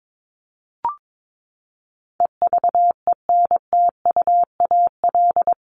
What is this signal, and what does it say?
Morse code, heard as a keyed tone, about 22 wpm, 700 Hz. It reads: EVENTUAL
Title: LY Suffix - Top 100 Words with Base Words 22wpm